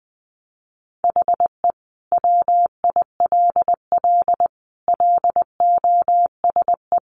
Morse code, heard as a keyed tone, about 20 wpm, 700 Hz. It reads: HE WILL LOSE
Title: Sentences from Top 1100 Words 20wpm